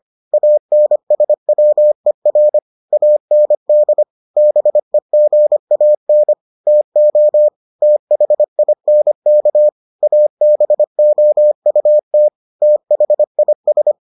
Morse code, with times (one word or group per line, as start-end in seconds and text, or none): 0.3-2.6 ANSWER
2.9-4.0 AND
4.4-6.3 BEGAN
6.7-7.5 TO
7.8-9.7 THINK
10.0-12.3 ABOUT
12.6-13.9 THIS